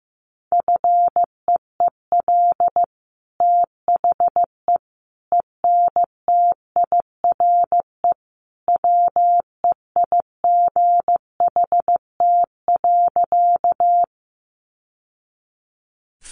Morse code, heard as a keyed tone, about 15 wpm, 700 Hz. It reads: FEEL THE ENTIRE WEIGHT.
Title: Sentences from Top 800 Words 15wpm